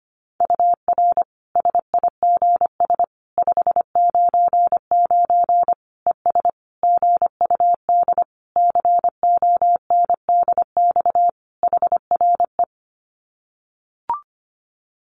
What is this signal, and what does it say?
Morse code, heard as a keyed tone, about 25 wpm, 700 Hz. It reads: UR SIGS 599 ES GUD CONDX HRE
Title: Common QSO Phrases 25wpm